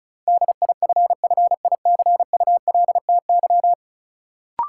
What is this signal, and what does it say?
Morse code, heard as a keyed tone, about 36 wpm, 700 Hz. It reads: DIFFICULTY
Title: Sentences from Top 1500 Words - New Word Drill Alt - 35wpm